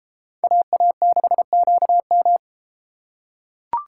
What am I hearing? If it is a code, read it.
Morse code, heard as a keyed tone, about 32 wpm, 700 Hz. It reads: AA6QM